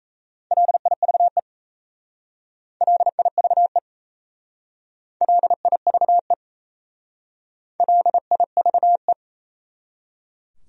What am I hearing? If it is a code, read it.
Morse code, heard as a keyed tone, about 37 wpm, 700 Hz. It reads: LIVE LIVE LIVE LIVE